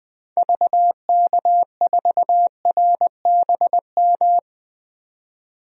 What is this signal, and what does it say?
Morse code, heard as a keyed tone, about 20 wpm, 700 Hz. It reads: VK4RBM
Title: DXCC-100 Call Signs - 6 Characters - 20wpm